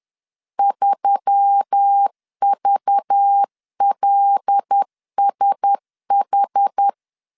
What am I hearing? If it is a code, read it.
Morse code, heard as a keyed tone, about 10 wpm, 800 Hz. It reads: 3VLSH
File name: LEZIONI / 6 PKX0 / 6 PK 10wpm 2436